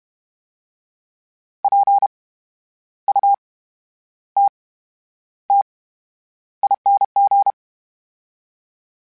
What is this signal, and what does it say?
Morse code, heard as a keyed tone, about 32 wpm, spaced as for 10 wpm, 800 Hz. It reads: PUTTING
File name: Special INGs / Special INGs-mp3s / VerySlowToFaster Ing @30